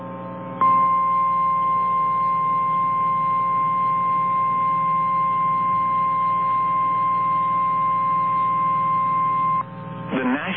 Emergency Alert System (SAME) tones (the 1050 Hz attention tone). It is heard from 0.6 to 9.6 s.
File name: eas / KNCYKBIE / KNCYKBIE